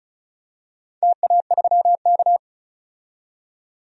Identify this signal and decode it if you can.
Morse code, heard as a keyed tone, about 36 wpm, 700 Hz. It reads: TA3K
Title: DXCC-100 Call Signs - 4 Characters - 35wpm